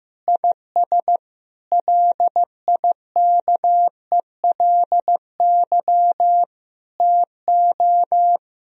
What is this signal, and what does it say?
Morse code, heard as a keyed tone, about 15 wpm, 700 Hz. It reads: IS LIKELY TO
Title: Sentences from Top 1000 Words - Review Part 4 - 15wpm